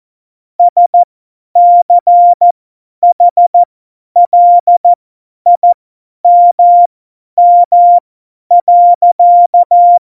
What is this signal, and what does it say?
Morse code, heard as a keyed tone, about 14 wpm, spaced as for 10 wpm, 700 Hz. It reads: SCHLIMM.
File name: Morsekurs / Witze / Witz 27. August 2019, 50 F70